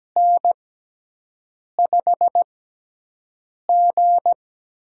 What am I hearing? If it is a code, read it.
Morse code, heard as a keyed tone, about 17 wpm, spaced as for 6 wpm, 700 Hz. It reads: N5G
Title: USA Call Signs - 4 Characters - 6x Character Spacing - 17wpm